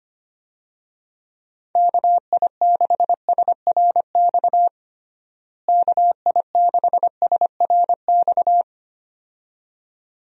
Morse code, heard as a keyed tone, about 25 wpm, 700 Hz. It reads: KI6SRX KI6SRX
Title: USA Call Signs - 6 Characters - Repeated 2x - 25wpm